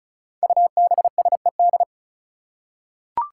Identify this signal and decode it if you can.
Morse code, heard as a keyed tone, about 34 wpm, 700 Hz. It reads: UBSED
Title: Five Challenging Letter-Number Combinations 35wpm